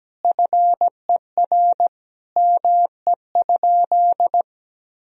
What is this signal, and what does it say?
Morse code, heard as a keyed tone, about 17 wpm, 700 Hz. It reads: FER ME?
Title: Sentences from Top 200 Words and Abbreviations 17wpm